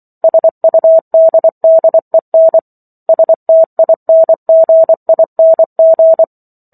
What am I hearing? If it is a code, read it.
Morse code, heard as a keyed tone, about 24 wpm, 650 Hz. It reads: SUDDEN STINGING